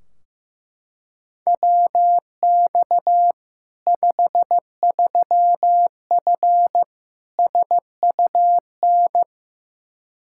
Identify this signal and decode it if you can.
Morse code, heard as a keyed tone, about 15 wpm, 700 Hz. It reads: WX 53F SUN
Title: Practice QSO - Exchange 2 - Info - 15wpm